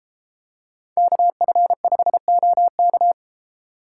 Morse code, heard as a keyed tone, about 30 wpm, 700 Hz. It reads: KF5OK